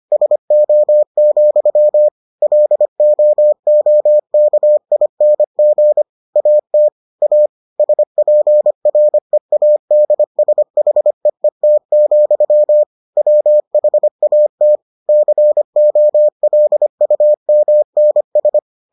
Morse code, heard as a keyed tone, about 25 wpm, 600 Hz. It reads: SO, LOOKING AT A SPREADSHEET, WHAT COLUMNS